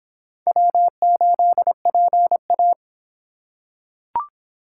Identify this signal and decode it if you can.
Morse code, heard as a keyed tone, about 25 wpm, 700 Hz. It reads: W8PA